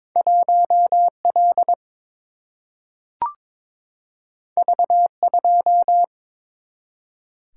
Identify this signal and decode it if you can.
Morse code, heard as a keyed tone, about 22 wpm, 700 Hz. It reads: 1L V2
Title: Single-Letter Single-Number 22wpm